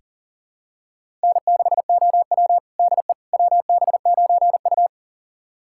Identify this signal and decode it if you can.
Morse code, heard as a keyed tone, about 37 wpm, 700 Hz. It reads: N6OW DE WB9U
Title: USA Callsign DE Callsign - 4 Characters - 40wpm